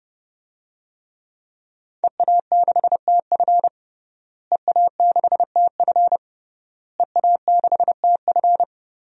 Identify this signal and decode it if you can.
Morse code, heard as a keyed tone, about 31 wpm, 700 Hz. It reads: EA6TF EA6TF EA6TF
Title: DXCC-100 Call Signs - 5 Characters - Repeated 3x - 30wpm